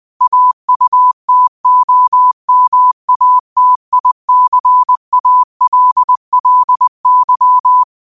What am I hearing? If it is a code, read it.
Morse code, heard as a keyed tone, about 20 wpm, 1000 Hz. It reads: AUTOMATICALLY
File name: qotdfiles / 20wpm